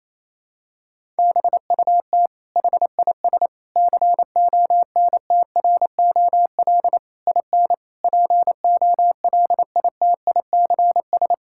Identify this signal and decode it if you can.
Morse code, heard as a keyed tone, about 28 wpm, 700 Hz. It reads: BUT HIS CONTROL IN POLITICS
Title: Sentences from Top 1000 Words - Review Part 6 - 28wpm